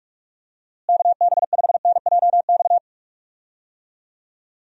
Morse code, heard as a keyed tone, about 41 wpm, 700 Hz. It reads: KB5NJX
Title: USA Call Signs - 6 Characters - Repeated 2x - 45wpm